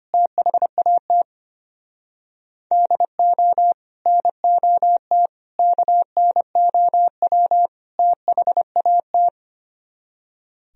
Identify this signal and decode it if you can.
Morse code, heard as a keyed tone, about 27 wpm, 700 Hz. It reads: THAT DO NOT KNOW THAT
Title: Speed-Racing - Sets of 4 Words 25wpm